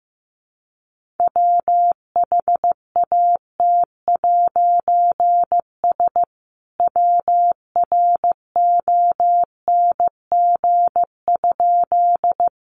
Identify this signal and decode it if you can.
Morse code, heard as a keyed tone, about 15 wpm, 700 Hz. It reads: WHAT'S WRONG?
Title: A Sound Of Thunder 15wpm